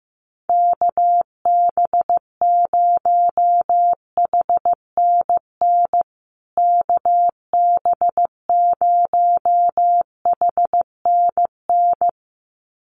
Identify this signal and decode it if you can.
Morse code, heard as a keyed tone, about 15 wpm, 700 Hz. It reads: KB0HNN KB0HNN
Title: USA Call Signs Repeated 2x 15wpm